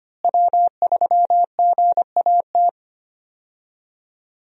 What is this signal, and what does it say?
Morse code, heard as a keyed tone, about 25 wpm, 700 Hz. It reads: W3GAT
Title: USA Call Signs - 5 Characters - Repeated 3x - 25wpm